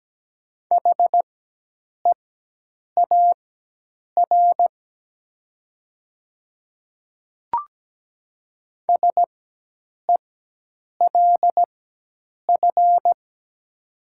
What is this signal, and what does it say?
Morse code, heard as a keyed tone, about 17 wpm, spaced as for 8 wpm, 700 Hz. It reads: HEAR SELF
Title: 4x Character Spacing - Top 300 Words - 17wpm